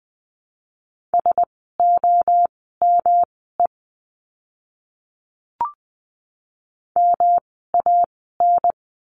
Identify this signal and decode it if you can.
Morse code, heard as a keyed tone, about 20 wpm, spaced as for 15 wpm, 700 Hz. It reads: SOME MAN
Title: 206 Lesson 33f - Letter j - Words 20wpm